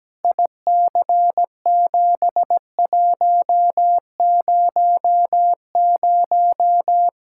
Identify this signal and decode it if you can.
Morse code, heard as a keyed tone, about 17 wpm, 700 Hz. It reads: IC7100
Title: K1USN SST QSO Practice 17wpm